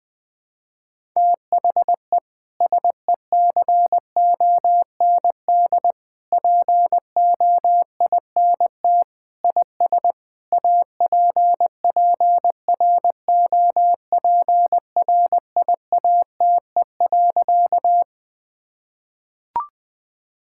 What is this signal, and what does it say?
Morse code, heard as a keyed tone, about 20 wpm, 700 Hz. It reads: THE SECOND POINT IS APPROPRIATE.
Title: Sentences from Top 1000 Words - Review Part 5 - 20wpm